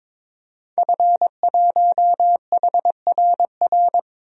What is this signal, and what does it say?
Morse code, heard as a keyed tone, about 22 wpm, 700 Hz. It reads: F1HRR
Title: DXCC-100 Call Signs - 5 Characters - Repeated 3x - 22wpm